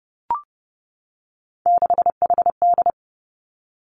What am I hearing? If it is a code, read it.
Morse code, heard as a keyed tone, about 30 wpm, 700 Hz. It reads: 6HD
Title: Three Challenging Letter-Number Combinations 30wpm